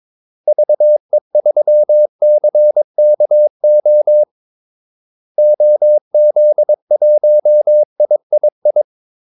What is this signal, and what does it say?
Morse code, heard as a keyed tone, about 22 wpm, 600 Hz. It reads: VE3CKO OZ1III
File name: Callsigns-22wpm